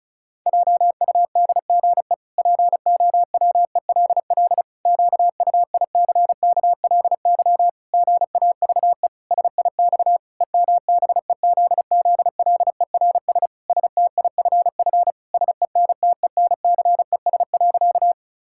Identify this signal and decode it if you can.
Morse code, heard as a keyed tone, about 36 wpm, 700 Hz. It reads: JUDGE POWELL QUICKLY GAVE SIX EMBEZZLERS STIFF SENTENCES.
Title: Pangram Challenge v2 - 35wpm